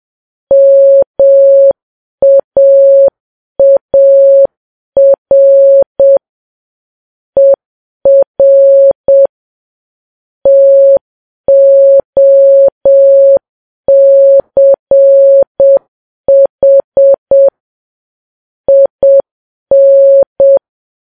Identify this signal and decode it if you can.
Morse code, heard as a keyed tone, about 7 wpm, 550 Hz. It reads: MAAR ER TOCH IN
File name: Files Web tls / Morse on9fv / morse7kps